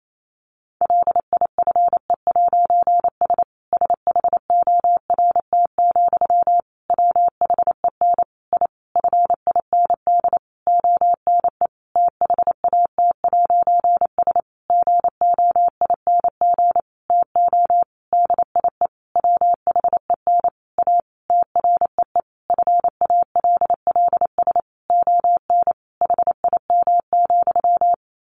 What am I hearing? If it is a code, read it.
Morse code, heard as a keyed tone, about 28 wpm, 700 Hz. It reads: LIFE'S SHORT, WHEN I FIND ONE THAT'S GOING TO DIE WHEN A TREE FALLS ON HIM,
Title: A Sound Of Thunder 28wpm